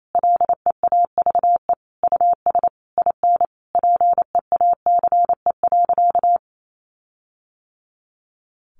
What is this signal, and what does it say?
Morse code, heard as a keyed tone, about 28 wpm, 700 Hz. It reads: LEAVE US IN PEACE.
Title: Sentences from Top 1500 Words 28wpm